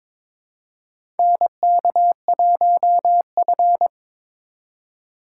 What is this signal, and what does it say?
Morse code, heard as a keyed tone, about 22 wpm, 700 Hz. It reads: NK1F
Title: USA Call Signs - 4 Characters - 3x Character Spacing - 22wpm